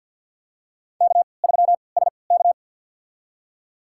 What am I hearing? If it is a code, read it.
Morse code, heard as a keyed tone, about 52 wpm, spaced as for 28 wpm, 700 Hz. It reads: K3SK
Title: USA Call Signs - 4 Characters - 3x Character Spacing - 50wpm